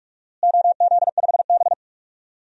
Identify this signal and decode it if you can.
Morse code, heard as a keyed tone, about 46 wpm, 700 Hz. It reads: OZ5B
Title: DXCC-100 Call Signs - 4 Characters - Repeated 3x - 45wpm